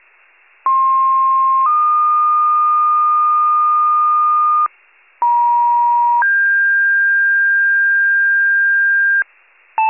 Two-tone page, heard as a one-second tone, then a three-second tone, about 1060 then 1220 Hz, about 950 then 1600 Hz, with noise. It is heard from 0.7 to 9.2 s.
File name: pager / 83 RESCUE DUTY / wcalert